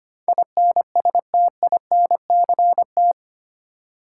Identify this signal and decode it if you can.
Morse code, heard as a keyed tone, about 25 wpm, 700 Hz. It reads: INSTINCT
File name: 8 Letter Words - First 3 Letters Given - 25wpm